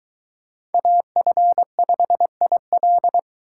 Morse code, heard as a keyed tone, about 23 wpm, 700 Hz. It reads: AF5IL